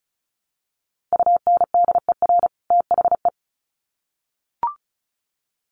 Morse code, heard as a keyed tone, about 34 wpm, 700 Hz. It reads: UNDER THE
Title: Sets of 2 Words 35wpm